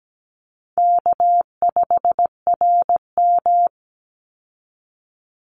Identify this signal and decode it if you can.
Morse code, heard as a keyed tone, about 17 wpm, 700 Hz. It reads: K5RM
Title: Easy Call Signs - 17wpm